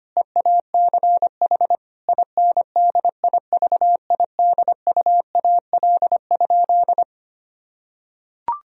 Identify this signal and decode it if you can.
Morse code, heard as a keyed tone, about 25 wpm, 700 Hz. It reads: EACH INDIVIDUAL?
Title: Sentences from Top 1000 Words - Review Part 4 - 25wpm